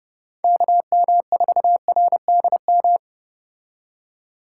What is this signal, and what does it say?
Morse code, heard as a keyed tone, about 31 wpm, 700 Hz. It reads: KM4RDM